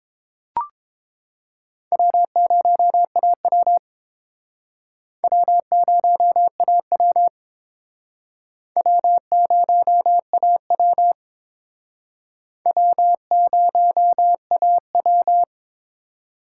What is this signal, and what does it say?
Morse code, heard as a keyed tone, about 28 wpm, 700 Hz. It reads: W0AW W0AW W0AW W0AW
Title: Speed Racing - USA Call Signs 22wpm